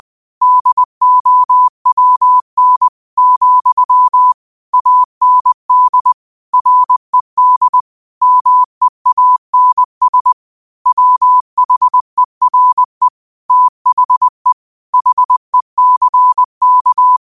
Morse code, heard as a keyed tone, about 20 wpm, 1000 Hz. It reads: DOWN, AND RED MEANS WHERE THE HECK
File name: qotdfiles / 20wpm